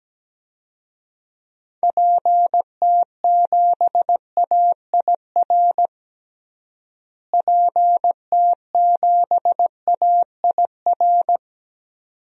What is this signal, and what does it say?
Morse code, heard as a keyed tone, about 17 wpm, 700 Hz. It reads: PT7AIR PT7AIR
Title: DXCC-100 Call Signs - 6 Characters - Repeated 2x - 17wpm